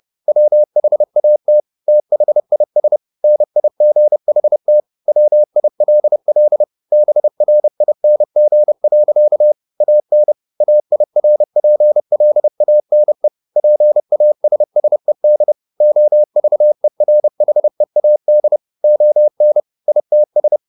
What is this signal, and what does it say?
Morse code, heard as a keyed tone, about 30 wpm, 600 Hz. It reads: WHAT THIS NIGHT WILL BRING. AN AIRPLANE PASSED OVERHEAD ON ITS